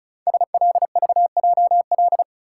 Morse code, heard as a keyed tone, about 36 wpm, 700 Hz. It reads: SLVJL
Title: Code Groups - 5 Letters - 35wpm